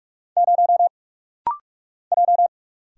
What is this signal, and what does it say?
Morse code, heard as a keyed tone, about 45 wpm, 700 Hz. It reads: 0 J
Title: Single Letter-Number - Rapid-Fire 45wpm